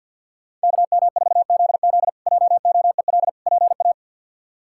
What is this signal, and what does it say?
Morse code, heard as a keyed tone, about 51 wpm, 700 Hz. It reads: KM4ZZ JOEL PA
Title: K1USN SST QSO Practice - 50wpm